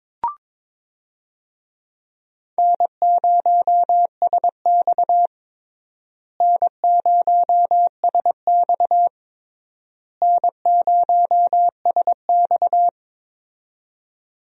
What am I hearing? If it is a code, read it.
Morse code, heard as a keyed tone, about 22 wpm, 700 Hz. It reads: N0SX N0SX N0SX
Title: USA Call Signs - 4 Characters - Repeated 3x - 22wpm